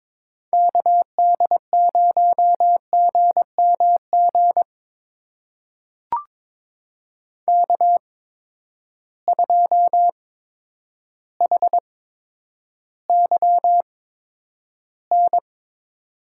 Morse code, heard as a keyed tone, about 22 wpm, spaced as for 10 wpm, 700 Hz. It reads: KD0GMG K2HYN
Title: USA Call Signs - 8x Character Spacing - 22wpm